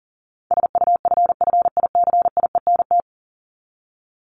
Morse code, heard as a keyed tone, about 40 wpm, 700 Hz. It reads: SUFFICIENT
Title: LY Suffix - Top 100 Words with Base Words 40wpm